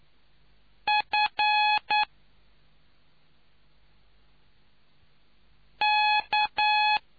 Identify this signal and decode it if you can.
Morse code, heard as a keyed tone, about 9 wpm, 840 Hz. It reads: F K